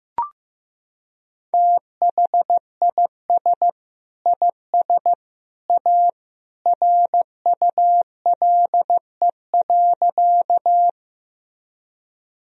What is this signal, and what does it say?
Morse code, heard as a keyed tone, about 15 wpm, 700 Hz. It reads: THIS IS A RULE.
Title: Sentences from Top 1000 Words - Review Part 2 - 15wpm